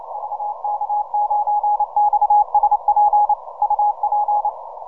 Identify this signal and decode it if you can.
Morse code, heard as a keyed tone, about 28 wpm, 830 Hz. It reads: VK9XSP UP
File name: simple-file-list / VK9XSP 80MCW